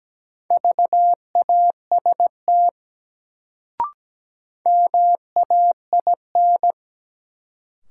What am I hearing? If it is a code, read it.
Morse code, heard as a keyed tone, about 17 wpm, 700 Hz. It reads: VAST MAIN
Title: Sentences from Top 1300 Words - New Word Drill Alt - 17wpm